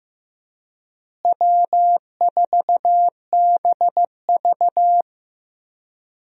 Morse code, heard as a keyed tone, about 15 wpm, 700 Hz. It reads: W4BV
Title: USA Call Signs - 4 Characters - Repeated 2x - 15wpm